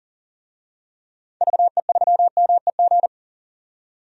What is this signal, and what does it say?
Morse code, heard as a keyed tone, about 40 wpm, 700 Hz. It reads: VE3MEG
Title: DXCC-100 Call Signs - 6 Characters - 40wpm